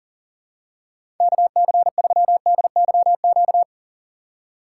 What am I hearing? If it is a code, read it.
Morse code, heard as a keyed tone, about 41 wpm, 700 Hz. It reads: KC3DYQ
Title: USA Call Signs - 3x Character Spacing - 40wpm